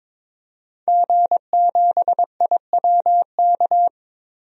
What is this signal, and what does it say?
Morse code, heard as a keyed tone, about 22 wpm, 700 Hz. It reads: G7IWK
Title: DXCC-100 Call Signs - 5 Characters - 22wpm